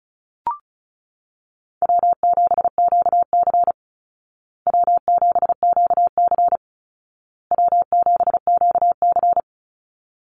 Morse code, heard as a keyed tone, about 34 wpm, 700 Hz. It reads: W7QC W7QC W7QC
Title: Easy Call Signs Repeated 3x - 35wpm